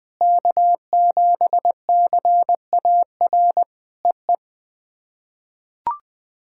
Morse code, heard as a keyed tone, about 20 wpm, 700 Hz. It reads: K7CAR EE
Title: POTA QSO Practice - 20wpm